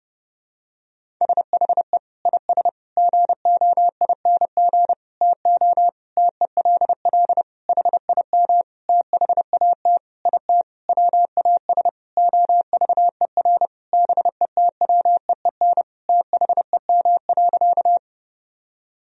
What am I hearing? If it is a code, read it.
Morse code, heard as a keyed tone, about 29 wpm, 700 Hz. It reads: SHE IS GOING TO TELL HIM THAT IT WAS OVER BETWEEN THEM.
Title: Sentences from Top 1000 Words - Review Part 1 - 30wpm